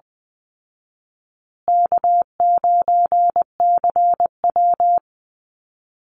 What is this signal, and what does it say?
Morse code, heard as a keyed tone, about 20 wpm, 700 Hz. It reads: K9CW
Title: Easy Call Signs Repeated 3x - 20wpm